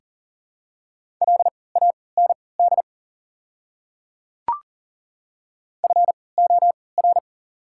Morse code, heard as a keyed tone, about 40 wpm, spaced as for 23 wpm, 700 Hz. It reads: LAND FOR
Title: 3x Character Spacing - Top 500 Words - 40wpm